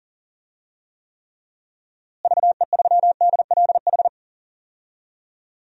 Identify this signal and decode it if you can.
Morse code, heard as a keyed tone, about 40 wpm, 700 Hz. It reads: VE3DLH
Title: DXCC-100 Call Signs - 6 Characters - Repeated 2x - 40wpm